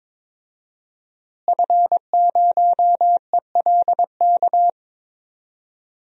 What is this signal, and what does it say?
Morse code, heard as a keyed tone, about 21 wpm, 700 Hz. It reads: F0ELK
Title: DXCC-100 Call Signs - 5 Characters - Repeated 2x - 22wpm